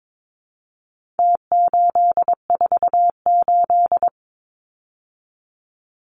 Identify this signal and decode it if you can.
Morse code, heard as a keyed tone, about 22 wpm, 700 Hz. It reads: T848
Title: Numbers - Contest Style - 4-Digit 22wpm